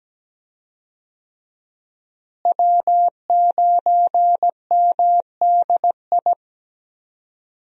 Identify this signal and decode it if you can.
Morse code, heard as a keyed tone, about 17 wpm, 700 Hz. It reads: W9MDI
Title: USA Call Signs - 5 Characters - Repeated 3x - 17wpm